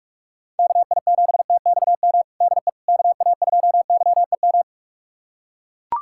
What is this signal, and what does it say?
Morse code, heard as a keyed tone, about 41 wpm, 700 Hz. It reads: KI7TXM DE KA2YEM
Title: USA Callsign DE Callsign - 6 Characters - 45wpm